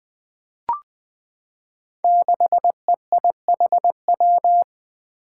Morse code, heard as a keyed tone, about 20 wpm, 700 Hz. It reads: 6EIHW
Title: Five Challenging Letter-Number Combinations 20wpm